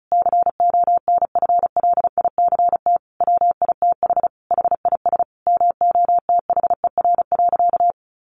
Morse code, heard as a keyed tone, about 34 wpm, 700 Hz. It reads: CONFLICT WITH HIS MOTHER.
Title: Sentences from Top 1300 Words 35wpm